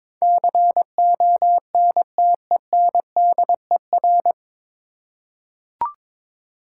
Morse code, heard as a keyed tone, about 22 wpm, 700 Hz. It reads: CONTENDER
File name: 9 Letter Words - First 3 Letters Given - 22wpm